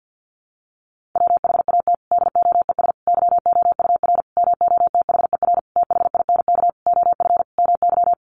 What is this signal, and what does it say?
Morse code, heard as a keyed tone, about 50 wpm, 700 Hz. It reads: WHAT DOES YOUR MOTHER THINK OF MY
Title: Sentences from Top 1400 Words 50wpm